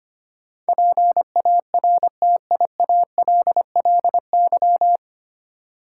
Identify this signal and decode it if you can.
Morse code, heard as a keyed tone, about 25 wpm, 700 Hz. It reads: PARTIALLY
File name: Nine Letter Words - 25wpm